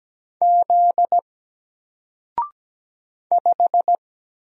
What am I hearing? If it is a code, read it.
Morse code, heard as a keyed tone, about 17 wpm, 700 Hz. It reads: Z 5
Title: Single Letter-Number Alt 17wpm